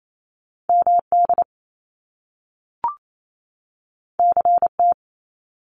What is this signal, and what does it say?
Morse code, heard as a keyed tone, about 28 wpm, 700 Hz. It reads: MD CT
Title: US States 28wpm